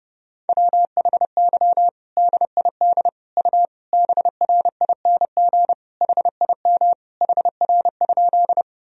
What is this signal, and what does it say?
Morse code, heard as a keyed tone, about 30 wpm, 700 Hz. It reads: WHY DID U BRING HIM HR?
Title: Sentences from Top 500 Words and Abbreviations 30wpm